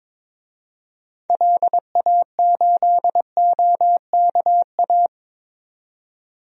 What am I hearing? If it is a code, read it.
Morse code, heard as a keyed tone, about 22 wpm, 700 Hz. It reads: LA8OKA
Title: DXCC-100 Call Signs - 6 Characters - 22wpm